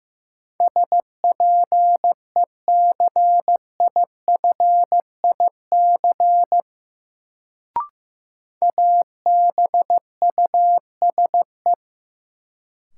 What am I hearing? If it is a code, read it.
Morse code, heard as a keyed tone, about 15 wpm, 700 Hz. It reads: SPECIFIC ABUSE
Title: Sentences from Top 1300 Words - New Word Drill Alt - 15wpm